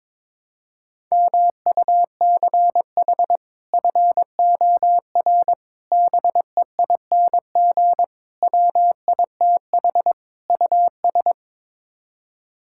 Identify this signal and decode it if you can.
Morse code, heard as a keyed tone, about 22 wpm, 700 Hz. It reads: MUCH FOR BEING WITH US
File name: practice / 22wpm / Build A Phrase - 5-Word Phrases - 22wpm